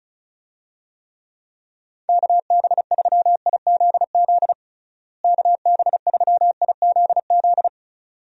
Morse code, heard as a keyed tone, about 36 wpm, 700 Hz. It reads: KB3IZZ KB3IZZ
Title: USA Call Signs - 6 Characters - Repeated 2x - 35wpm